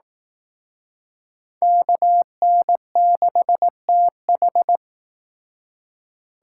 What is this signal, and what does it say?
Morse code, heard as a keyed tone, about 18 wpm, 700 Hz. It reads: KN6TH
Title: Speed Racing - USA 2x2 Call Signs 15wpm